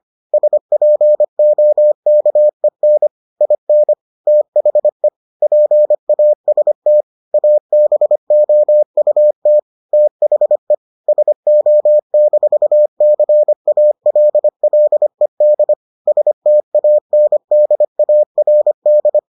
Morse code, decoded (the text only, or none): SPOKEN IN THE PAST ABOUT THE SO-CALLED STANDARD